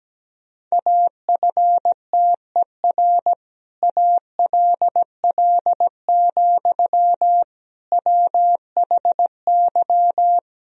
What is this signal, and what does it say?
Morse code, heard as a keyed tone, about 17 wpm, 700 Hz. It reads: AFTER ALL, WHY
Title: Sentences from Top 1000 Words - Review Part 2 - 17wpm